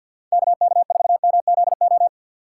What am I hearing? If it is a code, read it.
Morse code, heard as a keyed tone, about 47 wpm, 700 Hz. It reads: KK4MZO
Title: USA Call Signs - 6 Characters - 3x Character Spacing - 50wpm